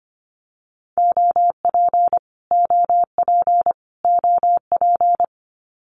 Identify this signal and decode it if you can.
Morse code, heard as a keyed tone, about 25 wpm, 700 Hz. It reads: OP OP OP
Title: Abbreviations Repeated 3x - 25wpm